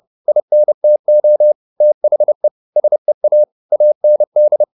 Morse code, heard as a keyed tone, about 30 wpm, 600 Hz. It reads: INTO THE SEA AND